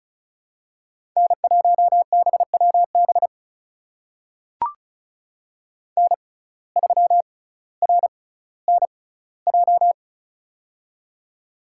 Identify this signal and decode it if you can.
Morse code, heard as a keyed tone, about 36 wpm, spaced as for 18 wpm, 700 Hz. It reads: N1BWB N3RNJ
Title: USA Call Signs - 5 Characters - 6x Character Spacing - 35wpm